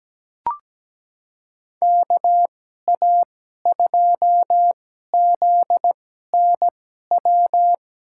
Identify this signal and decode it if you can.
Morse code, heard as a keyed tone, about 17 wpm, spaced as for 12 wpm, 700 Hz. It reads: KA2ZNW
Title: USA Call Signs - 6 Characters - 2x Character Spacing - 17wpm